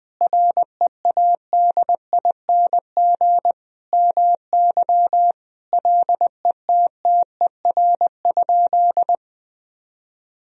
Morse code, heard as a keyed tone, about 20 wpm, 700 Hz. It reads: READING MY LETTER?
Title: Sentences from Top 1000 Words - Review Part 3 - 20wpm